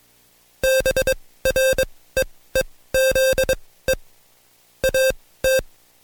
Morse code, heard as a keyed tone, about 22 wpm, spaced as for 16 wpm, 530 Hz. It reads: BREEZE AT
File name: Code Quick / AudioDownld2 / Seg134-Stepper7 15WPM